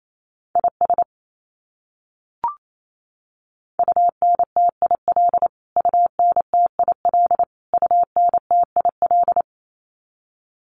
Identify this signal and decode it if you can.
Morse code, heard as a keyed tone, about 28 wpm, 700 Hz. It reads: IS UNTIL UNTIL UNTIL
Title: Top 500 Words Repeated 28wpm